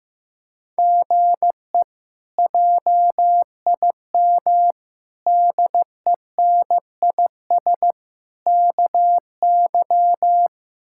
Morse code, heard as a keyed tone, about 15 wpm, 700 Hz. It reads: GE JIM DENIS KY
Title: K1USN SST QSO Practice - 15wpm